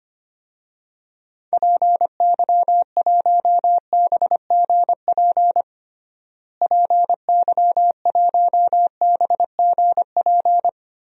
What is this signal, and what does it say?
Morse code, heard as a keyed tone, about 25 wpm, 700 Hz. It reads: PY1BGP PY1BGP
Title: DXCC-100 Call Signs - 6 Characters - Repeated 3x - 25wpm